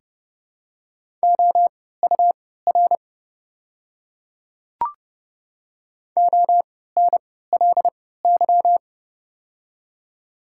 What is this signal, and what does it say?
Morse code, heard as a keyed tone, about 32 wpm, spaced as for 17 wpm, 690 Hz. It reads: OUR ONLY
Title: 3x Character Spacing - Top 200 Words - 30wpm